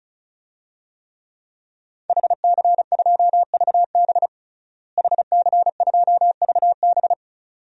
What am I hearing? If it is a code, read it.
Morse code, heard as a keyed tone, about 36 wpm, 700 Hz. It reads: HC2VB HC2VB
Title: DXCC-100 Call Signs - 5 Characters - Repeated 2x - 35wpm